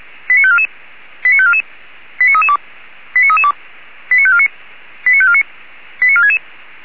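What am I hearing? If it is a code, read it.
Five-tone selective call (ZVEI), heard as quick runs of five tones, tones of about 70 ms. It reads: 89544 89544 89332 89332 89549 89549 89466
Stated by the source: ZVEI five-tone call received on SDR by feuerwehr-orlamuende (fire brigade Orlamünde).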